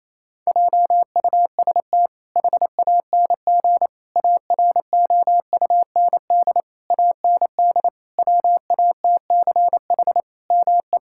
Morse code, heard as a keyed tone, about 27 wpm, 700 Hz. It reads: JUST HANG AROUND AND WATCH ME
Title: Sentences from Top 1600 Words 28wpm